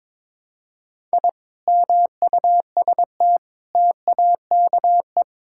Morse code, heard as a keyed tone, about 22 wpm, 700 Hz. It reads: I MUST TAKE
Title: Sentences from Top 1000 Words - Review Part 6 - 22wpm